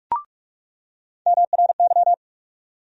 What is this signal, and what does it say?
Morse code, heard as a keyed tone, about 43 wpm, 700 Hz. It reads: MRY
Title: USA Call Sign Postfixes - 3 Letters 45wpm